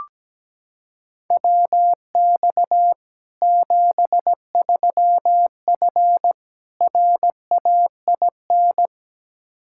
Morse code, heard as a keyed tone, about 17 wpm, 700 Hz. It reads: WX 73F RAIN
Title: Practice QSO - Exchange 2 - Info - 17wpm